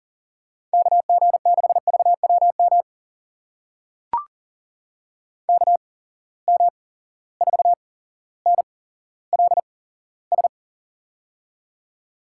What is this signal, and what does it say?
Morse code, heard as a keyed tone, about 40 wpm, spaced as for 17 wpm, 700 Hz. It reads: KG6VWM KM4NLS
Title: USA Call Signs - 6 Characters - 8x Character Spacing - 40wpm